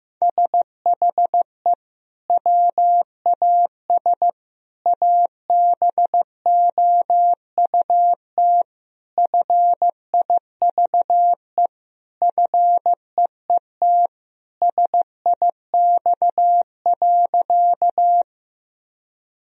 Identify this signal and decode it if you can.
Morse code, heard as a keyed tone, about 16 wpm, 700 Hz. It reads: SHE WAS ABOUT FIVE FEET SIX.
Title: Sentences from Top 1000 Words - Review Part 2 - 15wpm